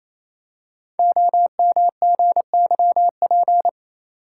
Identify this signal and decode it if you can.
Morse code, heard as a keyed tone, about 28 wpm, 700 Hz. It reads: OMGYP
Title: Code Groups - 5 Letters - 28wpm